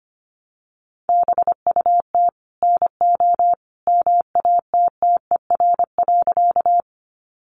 Morse code, heard as a keyed tone, about 25 wpm, 700 Hz. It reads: BUT NO MATTER.
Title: Sentences from Top 700 Words 25wpm